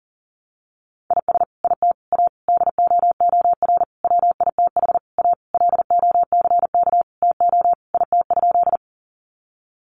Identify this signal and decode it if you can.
Morse code, heard as a keyed tone, about 37 wpm, 700 Hz. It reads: IS IT A DOOR WITH A LOCK TO IT?
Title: Sentences from Top 1300 Words 40wpm